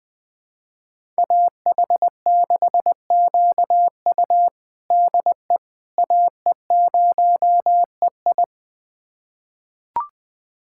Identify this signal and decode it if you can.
Morse code, heard as a keyed tone, about 20 wpm, 700 Hz. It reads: AH6QU DE AE0EI
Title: USA Callsign DE Callsign - 5 Characters - 20wpm